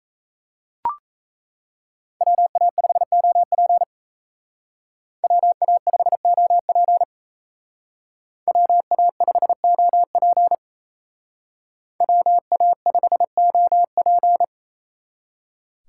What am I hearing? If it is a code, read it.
Morse code, heard as a keyed tone, about 36 wpm, 700 Hz. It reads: WA5OP WA5OP WA5OP WA5OP